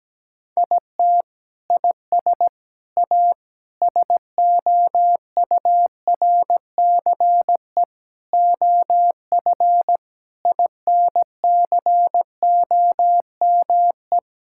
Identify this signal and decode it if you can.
Morse code, heard as a keyed tone, about 17 wpm, 700 Hz. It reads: IT IS A SOURCE OF INCOME
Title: Sentences from Top 1000 Words - Review Part 5 - 17wpm